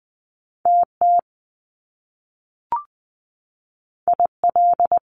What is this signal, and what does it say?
Morse code, heard as a keyed tone, about 20 wpm, 700 Hz. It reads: TT IL
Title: Top Two Letter Combinations in Words 20wpm